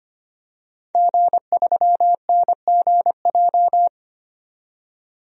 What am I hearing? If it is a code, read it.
Morse code, heard as a keyed tone, about 25 wpm, 700 Hz. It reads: G3NGJ
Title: DXCC-100 Call Signs - 5 Characters - 25wpm